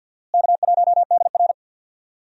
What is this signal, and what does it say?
Morse code, heard as a keyed tone, about 47 wpm, 690 Hz. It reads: K1DR
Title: USA Call Signs - 4 Characters - Repeated 2x - 50wpm